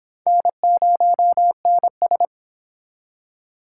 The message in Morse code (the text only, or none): N0NS